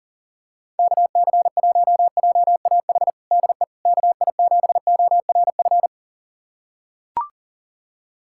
Morse code, heard as a keyed tone, about 40 wpm, 700 Hz. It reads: KC1JAH DE KI7ORF
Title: USA Callsign DE Callsign 40wpm